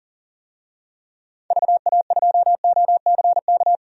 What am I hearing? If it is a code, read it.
Morse code, heard as a keyed tone, about 38 wpm, 700 Hz. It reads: VA2OCK
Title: DXCC-100 Call Signs - 6 Characters - 40wpm